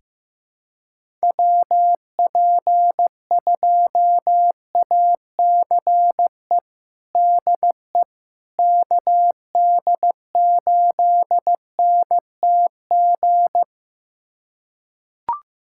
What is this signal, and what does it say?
Morse code, heard as a keyed tone, about 15 wpm, 700 Hz. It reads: WP2ACE DE KD8NTG
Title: USA Callsign DE Callsign - 6 Characters - 15wpm